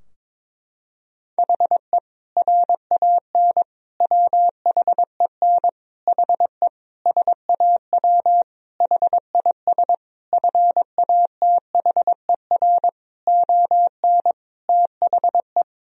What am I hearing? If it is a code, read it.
Morse code, heard as a keyed tone, about 22 wpm, 700 Hz. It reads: HE RAN WHEN HE SAW HIS FATHER ON THE
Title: Sentences from Top 1000 Words - Review Part 3 - 22wpm